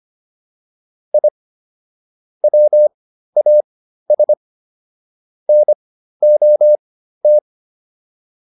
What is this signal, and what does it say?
Morse code, heard as a keyed tone, about 25 wpm, spaced as for 13 wpm, 600 Hz. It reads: I WAS NOT